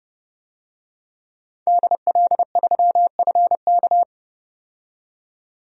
Morse code, heard as a keyed tone, about 30 wpm, 700 Hz. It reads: DL3FK
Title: DXCC-100 Call Signs - 5 Characters - 30wpm